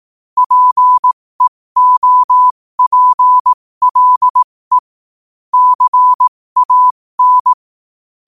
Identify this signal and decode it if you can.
Morse code, heard as a keyed tone, about 18 wpm, spaced as for 15 wpm, 1000 Hz. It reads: PEOPLE CAN
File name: qotdfiles / 15wpm